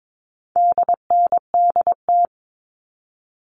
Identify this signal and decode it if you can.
Morse code, heard as a keyed tone, about 22 wpm, 700 Hz. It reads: DNDT
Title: All Cut Numbers - 4-Digit 22wpm